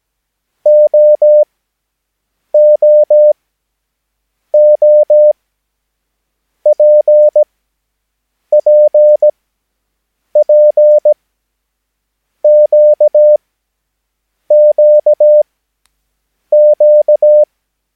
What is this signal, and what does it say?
Morse code, heard as a keyed tone, about 17 wpm, spaced as for 6 wpm, 600 Hz. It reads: OOOPPPQQQ
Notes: Övningsmaterial med alfabetet A till Ö samt siffror 0 till 9 enligt International/Continental Morse Code.